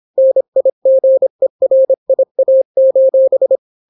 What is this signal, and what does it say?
Morse code, heard as a keyed tone, about 26 wpm, 530 Hz. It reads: NIGERIA: